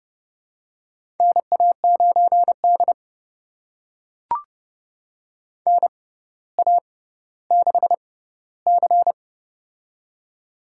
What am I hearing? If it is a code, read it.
Morse code, heard as a keyed tone, about 30 wpm, spaced as for 15 wpm, 700 Hz. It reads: NA9D NA6C
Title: USA Call Signs - 4 Characters - 6x Character Spacing - 30wpm